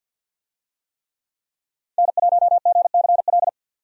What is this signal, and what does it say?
Morse code, heard as a keyed tone, about 48 wpm, 700 Hz. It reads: N1GCL
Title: USA Call Signs - 5 Characters - 50wpm